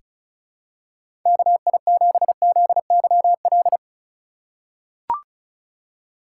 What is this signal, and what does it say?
Morse code, heard as a keyed tone, about 36 wpm, 700 Hz. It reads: KI7ZYL
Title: USA Call Signs - 6 Characters - 4x Character Spacing - 35wpm